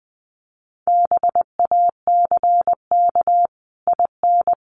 Morse code, heard as a keyed tone, about 20 wpm, 700 Hz. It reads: BACK IN
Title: Sets of 5 Words Encore 20wpm